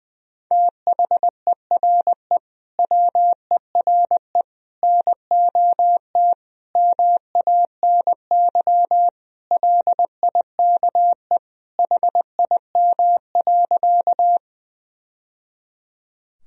Morse code, heard as a keyed tone, about 21 wpm, 700 Hz. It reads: THERE WERE NOT MANY LIKE HIM.
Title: Sentences from Top 1000 Words - Review Part 1 - 20wpm